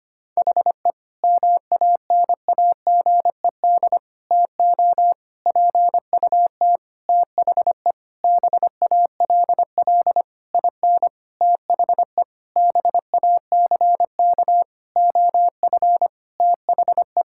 Morse code, heard as a keyed tone, about 25 wpm, 700 Hz. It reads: HE MANAGED TO PUT THE BALL IN THE BACK OF THE
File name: practice / 25wpm / Sentences with 9-Word Phrases - 25wpm